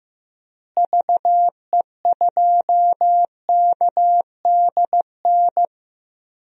Morse code, heard as a keyed tone, about 15 wpm, 700 Hz. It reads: VE2KDN
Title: DXCC-100 Call Signs - 6 Characters - Repeated 3x - 15wpm